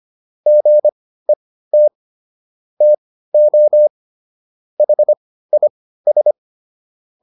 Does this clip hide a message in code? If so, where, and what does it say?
Yes: Morse code from 0.5 to 6.3 s, heard as a keyed tone, about 25 wpm, spaced as for 15 wpm, 600 Hz. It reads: GET TO HIS